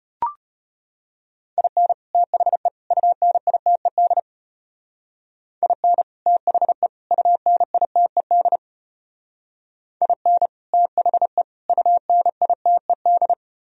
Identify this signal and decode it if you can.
Morse code, heard as a keyed tone, about 31 wpm, 700 Hz. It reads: IN THE UNITED IN THE UNITED IN THE UNITED
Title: Speed-Racing - Sets of 3 Words 25wpm